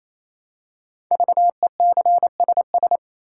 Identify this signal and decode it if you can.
Morse code, heard as a keyed tone, about 27 wpm, 700 Hz. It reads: VECSS
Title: Code Groups - 5 Letters - 28wpm